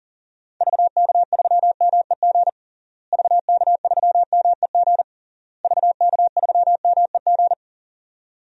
Morse code, heard as a keyed tone, about 40 wpm, 700 Hz. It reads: VK3MEG VK3MEG VK3MEG
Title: DXCC-100 Call Signs - 6 Characters - Repeated 3x - 40wpm